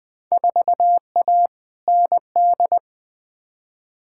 Morse code, heard as a keyed tone, about 21 wpm, 700 Hz. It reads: 4A ND
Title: Field Day Exchanges Encore - 20wpm